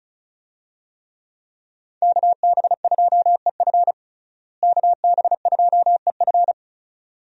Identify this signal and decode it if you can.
Morse code, heard as a keyed tone, about 34 wpm, 700 Hz. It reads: KB2EF KB2EF
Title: USA Call Signs - 5 Characters - Repeated 2x - 35wpm